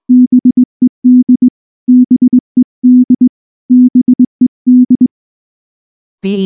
Morse code, heard as a keyed tone, about 20 wpm, 250 Hz. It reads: BED BED BED
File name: morse / morse-training-20WPM-7